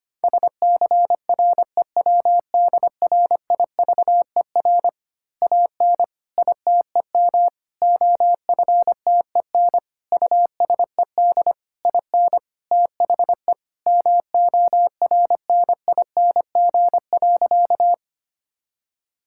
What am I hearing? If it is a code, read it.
Morse code, heard as a keyed tone, about 25 wpm, 700 Hz. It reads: SCREWDRIVER AN ITEM OFTEN USED IN THE MORNING.
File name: practice / 25wpm / Easy Words with Hints - 25wpm